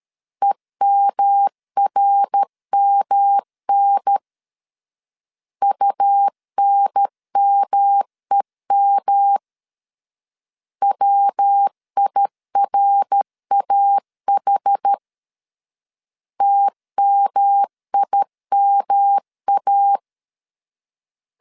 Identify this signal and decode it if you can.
Morse code, heard as a keyed tone, about 13 wpm, 800 Hz. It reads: EMRMN UNMEM WIRAH TMIMA
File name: CW / DET / 1 3 1500C S B 12wpm 2828